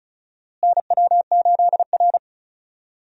Morse code, heard as a keyed tone, about 34 wpm, 700 Hz. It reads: NW8R
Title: USA Call Signs - 4 Characters - Repeated 2x - 35wpm